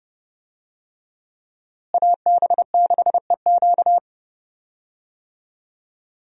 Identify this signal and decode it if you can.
Morse code, heard as a keyed tone, about 32 wpm, 700 Hz. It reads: AB6EQ
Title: USA Call Signs - 5 Characters - Repeated 3x - 30wpm